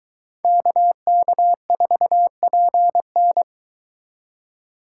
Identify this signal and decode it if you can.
Morse code, heard as a keyed tone, about 22 wpm, 700 Hz. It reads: KK4PN